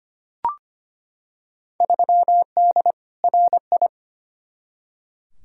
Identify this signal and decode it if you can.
Morse code, heard as a keyed tone, about 25 wpm, 700 Hz. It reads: 3D RI
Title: Field Day Exchanges Encore - 25wpm